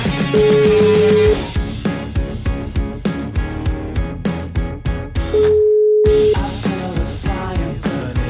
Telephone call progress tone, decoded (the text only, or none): ringback tone